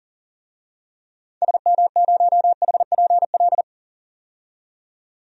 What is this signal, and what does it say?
Morse code, heard as a keyed tone, about 40 wpm, 700 Hz. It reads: SM0HPL
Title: DXCC-100 Call Signs - 6 Characters - 40wpm